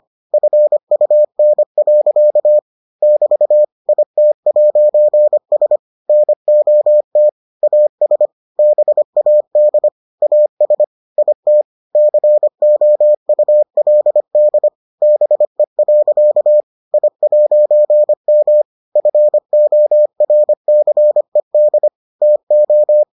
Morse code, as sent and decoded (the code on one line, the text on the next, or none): ..-. ..- -. .-.-.- / -...- / .. - .----. ... / -. --- - / .- ... / -... .- -.. / .- ... / .. - / -.-. --- ..- .-.. -.. / -... .-.-.- / .. .----. -- / ..-. --- .-. -.-. -.. / - ---
FUN. <BT> IT'S NOT AS BAD AS IT COULD BE. I'M FORCED TO